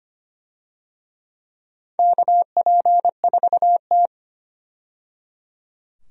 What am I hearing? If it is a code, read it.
Morse code, heard as a keyed tone, about 25 wpm, 700 Hz. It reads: KP4T
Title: DXCC-100 Call Signs - 4 Characters - 25wpm